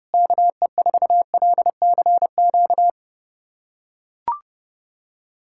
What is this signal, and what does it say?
Morse code, heard as a keyed tone, about 32 wpm, 700 Hz. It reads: KE4LCQ
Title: USA Callsign DE Callsign - 6 Characters - 30wpm